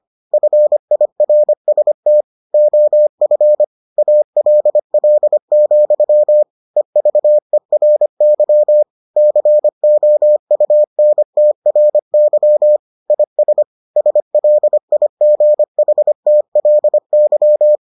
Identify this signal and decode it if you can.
Morse code, heard as a keyed tone, about 25 wpm, 600 Hz. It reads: FIRST OF ALL, EVERY COUNTRY IS SLIGHTLY